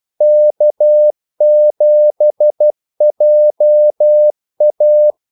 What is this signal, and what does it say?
Morse code, heard as a keyed tone, about 12 wpm, 600 Hz. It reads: K7JA
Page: 2 ekstra ordmellemrum mellem kaldesignalerne